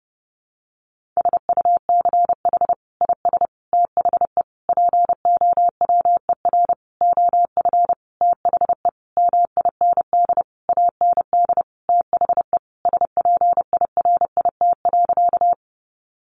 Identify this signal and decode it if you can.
Morse code, heard as a keyed tone, about 30 wpm, 700 Hz. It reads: SUCH IS THE POWER OF THE MIND AND THE SPIRIT.
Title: Sentences from Top 1400 Words 30wpm